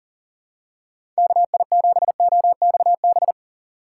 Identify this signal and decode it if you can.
Morse code, heard as a keyed tone, about 40 wpm, 700 Hz. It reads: KI7OXB
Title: USA Call Signs - 6 Characters - 4x Character Spacing - 40wpm